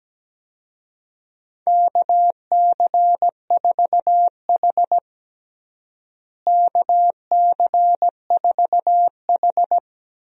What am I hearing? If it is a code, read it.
Morse code, heard as a keyed tone, about 17 wpm, 700 Hz. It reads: KC4H KC4H
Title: USA Call Signs - 4 Characters - Repeated 2x - 17wpm